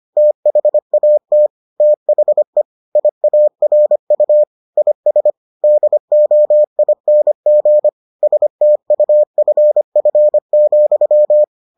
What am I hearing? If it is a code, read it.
Morse code, heard as a keyed tone, about 25 wpm, 600 Hz. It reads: THAT THE IARU IS DOING STUFF,